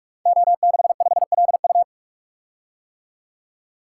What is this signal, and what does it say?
Morse code, heard as a keyed tone, about 44 wpm, 690 Hz. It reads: O65LU
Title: Code Groups - 5 Letter:Numbers - 45wpm